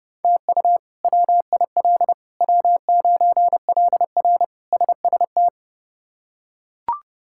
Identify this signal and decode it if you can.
Morse code, heard as a keyed tone, about 32 wpm, 700 Hz. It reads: TU WIL W9LR SST
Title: K1USN SST QSO Practice - 30wpm